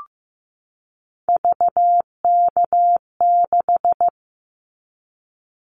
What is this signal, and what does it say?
Morse code, heard as a keyed tone, about 15 wpm, 700 Hz. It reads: VK6
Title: Callsign Prefixes for Top 100 DXCC - 15wpm